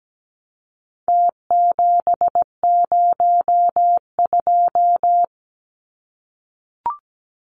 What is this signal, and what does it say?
Morse code, heard as a keyed tone, about 17 wpm, 700 Hz. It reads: T702
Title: Numbers - Contest Style - 4-Digit 17wpm